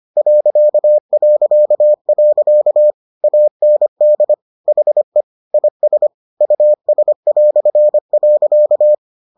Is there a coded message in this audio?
Yes: Morse code from 0.2 to 8.9 s, heard as a keyed tone, about 25 wpm, 600 Hz. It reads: ... AND HE IS US".